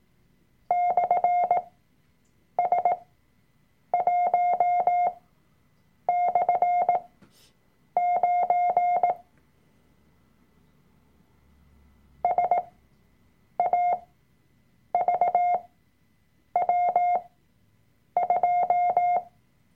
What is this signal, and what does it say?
Morse code, heard as a keyed tone, about 18 wpm, spaced as for 7 wpm, 710 Hz. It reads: /S1/9 SAVW2
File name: Workshop Dateien / CW-Online-Kurs / Lektion 003